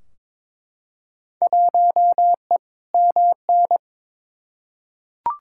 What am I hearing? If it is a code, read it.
Morse code, heard as a keyed tone, about 22 wpm, 700 Hz. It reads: 1E MN
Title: Field Day Exchanges - 22wpm